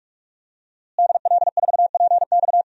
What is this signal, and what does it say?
Morse code, heard as a keyed tone, about 46 wpm, 700 Hz. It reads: DL4PX
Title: DXCC-100 Call Signs - 5 Characters - Repeated 3x - 45wpm